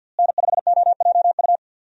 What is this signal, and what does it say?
Morse code, heard as a keyed tone, about 53 wpm, 700 Hz. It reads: N5OJU
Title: USA Call Signs - 5 Characters - Repeated 2x - 50wpm